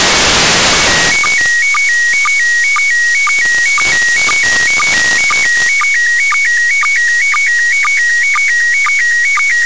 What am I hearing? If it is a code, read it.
SSTV (slow-scan television), PD 120.
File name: commit / satnogs 29407 short